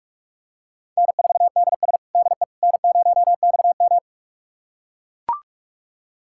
Morse code, heard as a keyed tone, about 41 wpm, 700 Hz. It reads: N4DS DE N0XM
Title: USA Callsign DE Callsign - 4 Characters - 45wpm